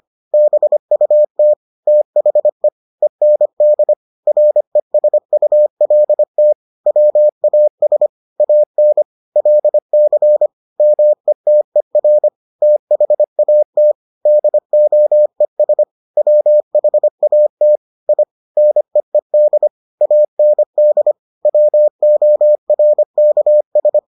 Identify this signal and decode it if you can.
Morse code, heard as a keyed tone, about 25 wpm, 600 Hz. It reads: BUT THE END RESULT WAS AN LC METER THAT DOES WHAT I NEED AND WORKS